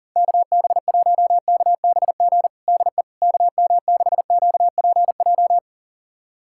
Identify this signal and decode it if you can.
Morse code, heard as a keyed tone, about 40 wpm, 700 Hz. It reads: KB1KBG DE KM6QPJ
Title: USA Callsign DE Callsign 40wpm